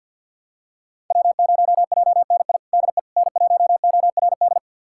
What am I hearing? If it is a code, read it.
Morse code, heard as a keyed tone, about 50 wpm, 700 Hz. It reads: W0JNI DE N1ORD
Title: USA Callsign DE Callsign - 5 Characters - 50wpm